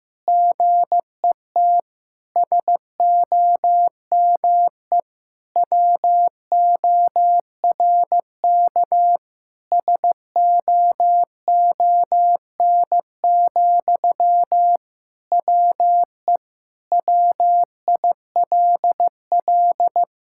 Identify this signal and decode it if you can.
Morse code, heard as a keyed tone, about 15 wpm, 700 Hz. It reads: GET SOME WORK SOON, WE WILL
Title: Sentences from Top 1000 Words - Review Part 6 - 15wpm